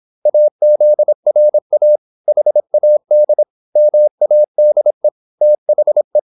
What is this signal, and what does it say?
Morse code, heard as a keyed tone, about 26 wpm, 600 Hz. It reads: AZRA HAD MADE THE
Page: Never 26wpm | CW med Gnister